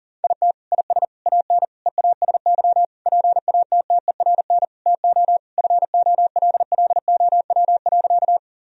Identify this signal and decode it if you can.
Morse code, heard as a keyed tone, about 40 wpm, 700 Hz. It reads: IT IS AN EASY PATTERN TO FOLLOW.
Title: Sentences from Top 1000 Words - Review Part 2 - 40wpm